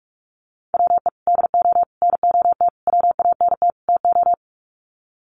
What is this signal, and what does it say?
Morse code, heard as a keyed tone, about 48 wpm, 700 Hz. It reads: WE DO NOT WANT TO
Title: Sets of 5 Words 45wpm